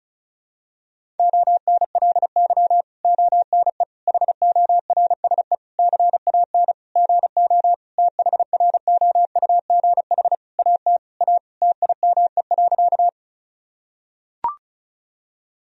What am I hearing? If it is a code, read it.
Morse code, heard as a keyed tone, about 34 wpm, 700 Hz. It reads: ONLY ONE HORSE CAN GO THROUGH AT A TIME.
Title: Sentences from Top 1000 Words - Review Part 1 - 35wpm